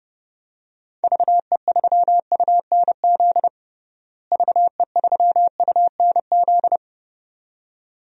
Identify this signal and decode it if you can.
Morse code, heard as a keyed tone, about 30 wpm, 700 Hz. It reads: VE3UNZ VE3UNZ
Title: DXCC-100 Call Signs - 6 Characters - Repeated 2x - 30wpm